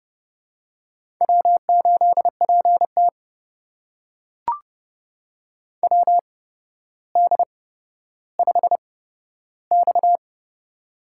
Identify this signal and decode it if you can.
Morse code, heard as a keyed tone, about 30 wpm, spaced as for 13 wpm, 700 Hz. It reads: W8PT WD5X
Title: USA Call Signs - 4 Characters - 8x Character Spacing - 30wpm